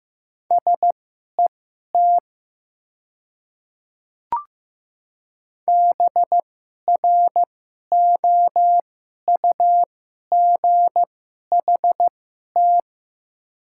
Morse code, heard as a keyed tone, about 15 wpm, spaced as for 11 wpm, 690 Hz. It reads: SET BROUGHT
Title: 2x Character Spacing - Top 500 Words - 15wpm